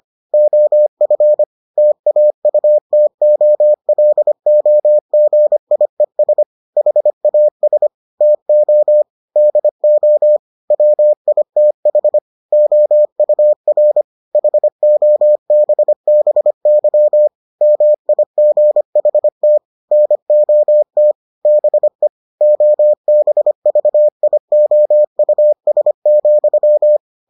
Morse code, heard as a keyed tone, about 25 wpm, 600 Hz. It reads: OF TAUTOLOGIES HAS TO DO WITH OUR HOBBY MIGHT NOT BE OBVIOUS,